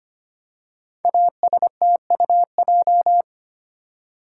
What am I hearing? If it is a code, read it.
Morse code, heard as a keyed tone, about 25 wpm, 700 Hz. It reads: ASTUJ
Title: Code Groups - 5 Letters - 25wpm